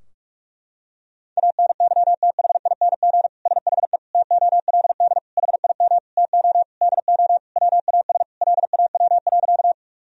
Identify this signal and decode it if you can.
Morse code, heard as a keyed tone, about 47 wpm, 700 Hz. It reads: ANYTHING SHE TOLD HIM TO DO WAS LAW.
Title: Sentences from Top 1000 Words - Review Part 3 - 45wpm